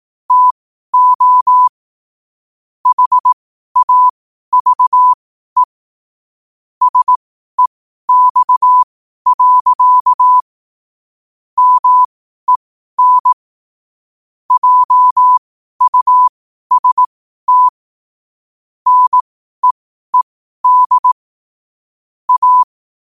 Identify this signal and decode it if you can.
Morse code, heard as a keyed tone, about 18 wpm, spaced as for 12 wpm, 1000 Hz. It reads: TO HAVE SEX. MEN JUST NEED A
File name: qotdfiles / 12wpm